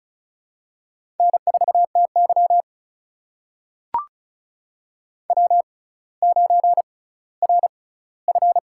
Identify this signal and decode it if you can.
Morse code, heard as a keyed tone, about 36 wpm, spaced as for 18 wpm, 700 Hz. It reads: N4TY W9RF
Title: USA Call Signs - 4 Characters - 6x Character Spacing - 35wpm